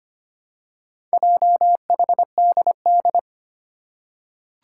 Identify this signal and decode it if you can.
Morse code, heard as a keyed tone, about 25 wpm, 700 Hz. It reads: JHDD
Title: Four Challenging Letter-Number Combinations 25wpm